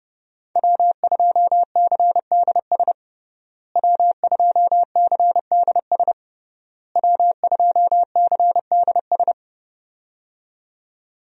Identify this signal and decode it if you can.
Morse code, heard as a keyed tone, about 30 wpm, 700 Hz. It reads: W2CDS W2CDS W2CDS
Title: USA Call Signs - 5 Characters - Repeated 3x - 30wpm